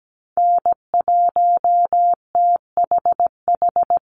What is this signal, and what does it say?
Morse code, heard as a keyed tone, about 17 wpm, 700 Hz. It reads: N1THH
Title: USA Call Signs 17wpm